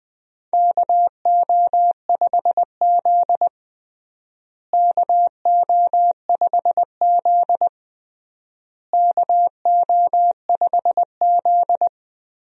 Morse code, heard as a keyed tone, about 20 wpm, 700 Hz. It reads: KO5Z KO5Z KO5Z
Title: USA Call Signs - 4 Characters - Repeated 3x - 20wpm